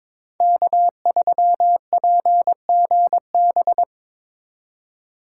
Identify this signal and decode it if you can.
Morse code, heard as a keyed tone, about 22 wpm, 700 Hz. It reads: K3PGB
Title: USA Call Signs - 5 Characters - 3x Character Spacing - 22wpm